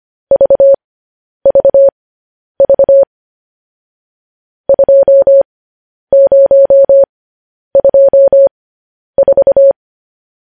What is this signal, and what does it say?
Morse code, heard as a keyed tone, about 25 wpm, spaced as for 10 wpm, 550 Hz. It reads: VVV 2024
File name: podcast / News headlines-1x-25x10